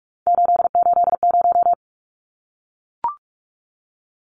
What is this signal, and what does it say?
Morse code, heard as a keyed tone, about 44 wpm, 700 Hz. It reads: 880
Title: Numbers - 3-Digit 45wpm